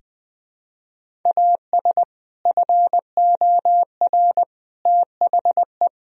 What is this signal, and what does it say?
Morse code, heard as a keyed tone, about 21 wpm, 700 Hz. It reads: AS FOR THE
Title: Sentences from Top 1000 Words - Review Part 1 - 20wpm